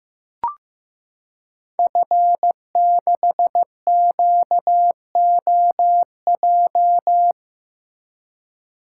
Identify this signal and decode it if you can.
Morse code, heard as a keyed tone, about 15 wpm, 700 Hz. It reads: F6QOJ
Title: Code Groups - 5 Letter:Numbers - 15wpm